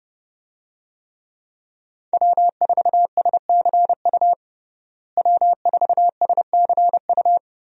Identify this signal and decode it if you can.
Morse code, heard as a keyed tone, about 30 wpm, 700 Hz. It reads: W4SCU W4SCU
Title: USA Call Signs - 5 Characters - Repeated 2x - 30wpm